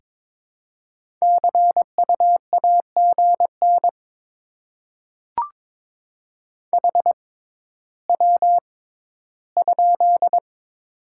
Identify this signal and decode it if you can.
Morse code, heard as a keyed tone, about 22 wpm, spaced as for 13 wpm, 700 Hz. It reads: CU AGN HW?
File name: Top 20 QSO Elements - 6x Character Spacing - 22wpm